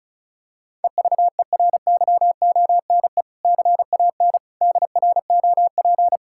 Morse code, heard as a keyed tone, about 34 wpm, 700 Hz. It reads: EVERYONE CAN DROP
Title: Sentences from Top 1600 Words 35wpm